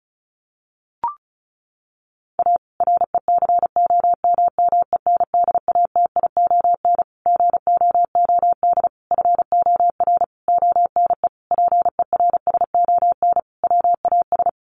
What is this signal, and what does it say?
Morse code, heard as a keyed tone, about 34 wpm, 700 Hz. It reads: A RECOMMENDATION GOOD FOR ONE PERSON WAS
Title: Sentences from Top 1400 Words 35wpm